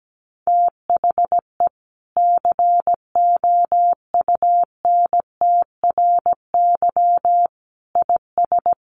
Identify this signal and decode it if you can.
Morse code, heard as a keyed tone, about 17 wpm, 700 Hz. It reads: THE COUNTRY IS
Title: Sentences from Top 1100 Words 17wpm